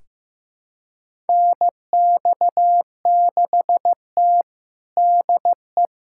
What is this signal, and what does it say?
Morse code, heard as a keyed tone, about 15 wpm, 700 Hz. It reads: NX6T DE
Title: USA Callsign DE Callsign - 4 Characters - 15wpm